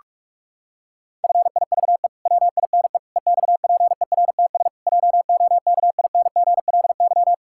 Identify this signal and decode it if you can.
Morse code, heard as a keyed tone, about 46 wpm, 700 Hz. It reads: FIVE WINE EXPERTS JOKINGLY
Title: Pangram Challenge v2 - 45wpm